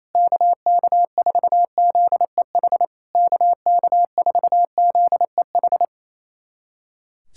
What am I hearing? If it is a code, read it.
Morse code, heard as a keyed tone, about 28 wpm, 690 Hz. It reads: KK4ZEH KK4ZEH
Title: SOTA QSO Practice 28wpm